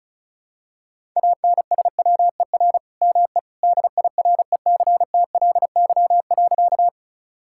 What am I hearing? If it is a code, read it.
Morse code, heard as a keyed tone, about 34 wpm, 700 Hz. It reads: ANSWER ME DIRECTLY.
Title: Sentences from Top 1000 Words - Review Part 5 - 35wpm